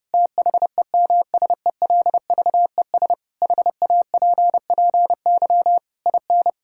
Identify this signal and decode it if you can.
Morse code, heard as a keyed tone, about 32 wpm, 700 Hz. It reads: THEMSELVES HAPPY IN
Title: Sentences from Top 1600 Words 30wpm